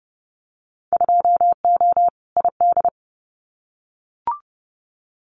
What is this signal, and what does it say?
Morse code, heard as a keyed tone, about 30 wpm, 700 Hz. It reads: 2O ID
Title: Winter Field Day Exchanges 30wpm